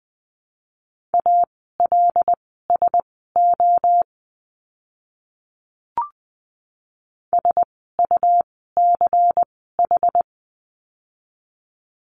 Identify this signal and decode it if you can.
Morse code, heard as a keyed tone, about 20 wpm, spaced as for 15 wpm, 700 Hz. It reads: ALSO SUCH
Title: 146 Lesson 24f - Letter y - Words 20wpm